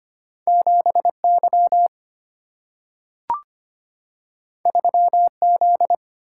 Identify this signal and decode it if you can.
Morse code, heard as a keyed tone, about 25 wpm, 700 Hz. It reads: 7Y 3Z
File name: Two Letter-Number Combinations - 25wpm